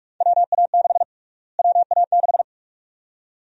Morse code, heard as a keyed tone, about 46 wpm, 700 Hz. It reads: WA6 WA6
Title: USA Call Sign Prefixes - Repeated 2x - 45wpm